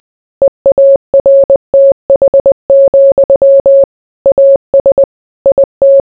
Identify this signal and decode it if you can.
Morse code, heard as a keyed tone, about 20 wpm, 550 Hz. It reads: EARTH, AS IT